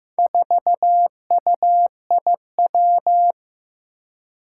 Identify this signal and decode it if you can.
Morse code, heard as a keyed tone, about 15 wpm, 700 Hz. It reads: 4UIW
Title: Four Challenging Letter-Number Combinations 15wpm